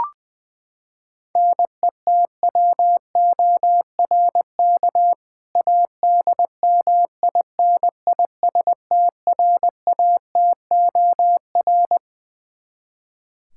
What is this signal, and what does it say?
Morse code, heard as a keyed tone, about 20 wpm, 700 Hz. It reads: NETWORK ADMINISTRATOR
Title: K1USN SST QSO Practice 20wpm